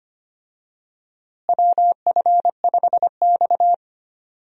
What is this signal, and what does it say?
Morse code, heard as a keyed tone, about 25 wpm, 700 Hz. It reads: WF5X
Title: USA Call Signs - 4 Characters - Repeated 3x - 25wpm